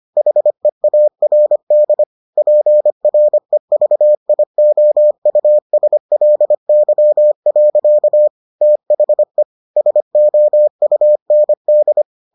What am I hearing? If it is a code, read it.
Morse code, heard as a keyed tone, about 25 wpm, 600 Hz. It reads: HEARD PREVIOUSLY. THE SOUND